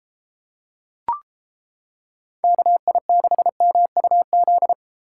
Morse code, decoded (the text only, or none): KI6MUZ